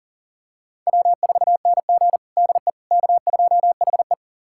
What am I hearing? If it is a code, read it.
Morse code, heard as a keyed tone, about 40 wpm, 700 Hz. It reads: W4NG DE K2HE
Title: USA Callsign DE Callsign - 4 Characters - 40wpm